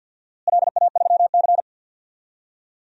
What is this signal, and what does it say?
Morse code, heard as a keyed tone, about 49 wpm, 700 Hz. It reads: LA3C
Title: DXCC-100 Call Signs - 4 Characters - Repeated 3x - 50wpm